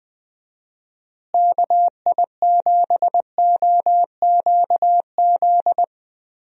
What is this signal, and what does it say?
Morse code, heard as a keyed tone, about 21 wpm, 700 Hz. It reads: KI7OQZ
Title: USA Call Signs - 6 Characters - Repeated 3x - 20wpm